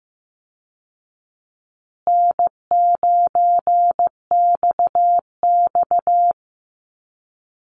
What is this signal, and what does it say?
Morse code, heard as a keyed tone, about 15 wpm, 700 Hz. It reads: N9XX
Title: Easy Call Signs Repeated 3x - 15wpm